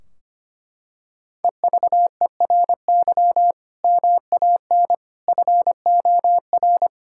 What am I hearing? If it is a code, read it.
Morse code, heard as a keyed tone, about 25 wpm, 700 Hz. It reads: EVERY MAN FOR
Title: Sentences from Top 1600 Words 25wpm